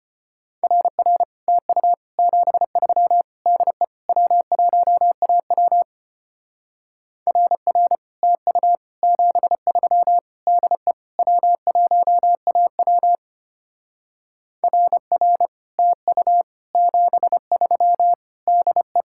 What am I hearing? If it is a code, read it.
Morse code, heard as a keyed tone, about 32 wpm, 700 Hz. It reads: RR TU 73 DE W1AW RR TU 73 DE W1AW RR TU 73 DE